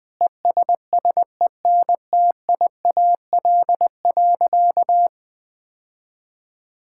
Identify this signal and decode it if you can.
Morse code, heard as a keyed tone, about 20 wpm, 700 Hz. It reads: ESSENTIAL.
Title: Sentences with 7-Word Phrases - 20wpm